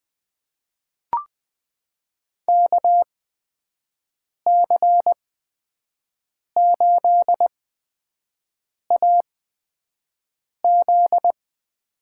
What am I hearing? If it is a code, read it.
Morse code, heard as a keyed tone, about 20 wpm, spaced as for 5 wpm, 700 Hz. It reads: KC8AZ
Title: USA Call Signs - 8x Character Spacing - 20wpm